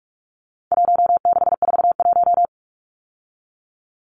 Morse code, heard as a keyed tone, about 44 wpm, 700 Hz. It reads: 1641
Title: Numbers - Contest Style - 4-Digit 45wpm